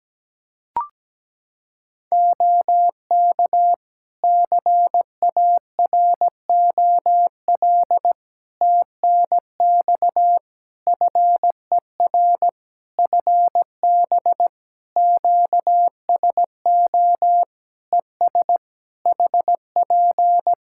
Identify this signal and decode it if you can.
Morse code, heard as a keyed tone, about 17 wpm, 700 Hz. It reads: OK CAROL TNX FER FB QSO ES HP
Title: K1USN SST QSO Practice 17wpm